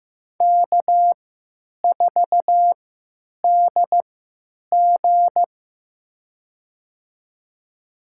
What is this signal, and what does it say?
Morse code, heard as a keyed tone, about 15 wpm, spaced as for 9 wpm, 700 Hz. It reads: K4DG
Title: USA Call Signs - 4 Characters - 3x Character Spacing - 15wpm